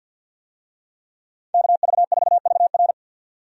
Morse code, heard as a keyed tone, about 50 wpm, 700 Hz. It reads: KV4VR
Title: USA Call Signs - 5 Characters - 50wpm